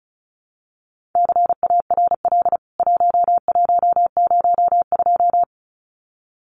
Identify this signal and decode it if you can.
Morse code, heard as a keyed tone, about 34 wpm, 700 Hz. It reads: CARL 1102
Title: CWT Exchange 35wpm